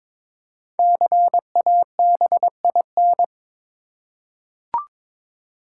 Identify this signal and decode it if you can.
Morse code, heard as a keyed tone, about 22 wpm, 700 Hz. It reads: CABIN
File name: 5 Letter Words - First 3 Letters Given - 22wpm